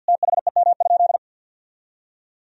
Morse code, heard as a keyed tone, about 50 wpm, 700 Hz. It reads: THEM?
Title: Sentences from Top 1000 Words - Review Part 4 - 50wpm